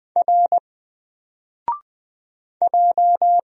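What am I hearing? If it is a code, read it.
Morse code, heard as a keyed tone, about 20 wpm, 700 Hz. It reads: R J
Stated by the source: Single letter and number Morse drill at 20 wpm, from N4LCD.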